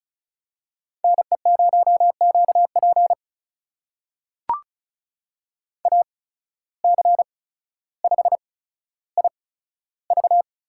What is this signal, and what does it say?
Morse code, heard as a keyed tone, about 34 wpm, spaced as for 15 wpm, 700 Hz. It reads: NE0QP AC5IV
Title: USA Call Signs - 8x Character Spacing - 35wpm